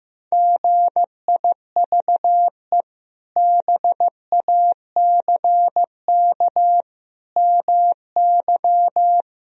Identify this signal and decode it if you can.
Morse code, heard as a keyed tone, about 15 wpm, 700 Hz. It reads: GIVE BACK MY
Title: Sentences from Top 1000 Words - Review Part 1 - 15wpm